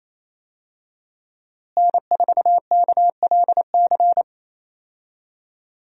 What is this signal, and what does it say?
Morse code, heard as a keyed tone, about 28 wpm, 700 Hz. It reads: N4KLC
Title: USA Call Signs - 5 Characters - Repeated 2x - 28wpm